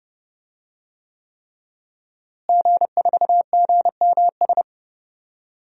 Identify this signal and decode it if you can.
Morse code, heard as a keyed tone, about 30 wpm, 700 Hz. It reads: G4GMS
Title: DXCC-100 Call Signs - 5 Characters - Repeated 3x - 30wpm